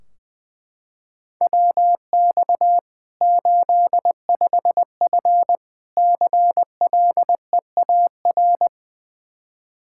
Morse code, heard as a keyed tone, about 20 wpm, 700 Hz. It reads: WX 85F CLEAR
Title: Practice QSO - Exchange 2 - Info - 20wpm